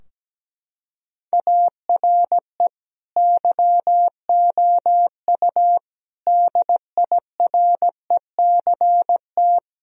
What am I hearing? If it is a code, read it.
Morse code, heard as a keyed tone, about 17 wpm, 700 Hz. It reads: ARE YOU DIRECT
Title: Sentences from Top 1000 Words - Review Part 4 - 17wpm